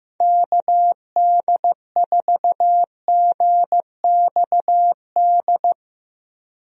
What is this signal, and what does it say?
Morse code, heard as a keyed tone, about 15 wpm, 700 Hz. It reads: KD4GXD
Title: USA Call Signs - 6 Characters - Repeated 2x - 15wpm